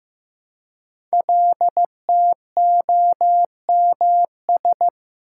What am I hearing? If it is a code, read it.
Morse code, heard as a keyed tone, about 15 wpm, 700 Hz. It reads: LTOMS
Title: Code Groups - 5 Letter:Numbers - 15wpm